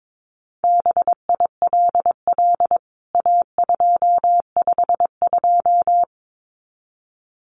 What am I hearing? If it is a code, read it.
Morse code, heard as a keyed tone, about 22 wpm, 700 Hz. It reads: BILL A252
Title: CWT Exchange 22wpm